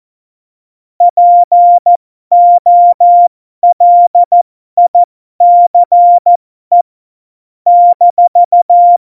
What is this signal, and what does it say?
Morse code, heard as a keyed tone, about 14 wpm, spaced as for 12 wpm, 700 Hz. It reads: POLICE -
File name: Morsekurs / Witze